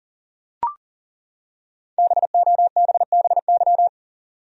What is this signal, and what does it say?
Morse code, heard as a keyed tone, about 40 wpm, 700 Hz. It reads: BOBBY
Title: USA Common Names - Max 5 Letters - 40wpm